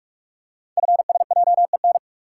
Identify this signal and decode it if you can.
Morse code, heard as a keyed tone, about 44 wpm, 690 Hz. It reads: FSJEN
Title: Code Groups - 5 Letters - 45wpm